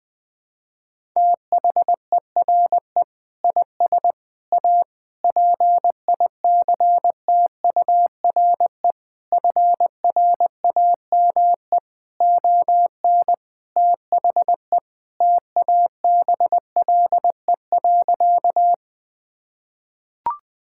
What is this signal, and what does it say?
Morse code, heard as a keyed tone, about 20 wpm, 700 Hz. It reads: THERE IS A PICTURE FRAME ON THE TABLE.
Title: Sentences from Top 1000 Words - Review Part 6 - 20wpm